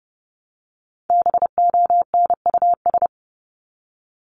Morse code, heard as a keyed tone, about 30 wpm, 700 Hz. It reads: BONUS
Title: Top 1751-2000 Words 30wpm